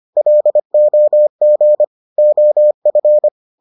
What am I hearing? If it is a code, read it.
Morse code, heard as a keyed tone, about 25 wpm, 600 Hz. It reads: LOG OF